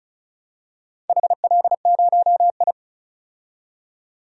Morse code, heard as a keyed tone, about 34 wpm, 700 Hz. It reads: HL0I
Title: DXCC-100 Call Signs - 4 Characters - 35wpm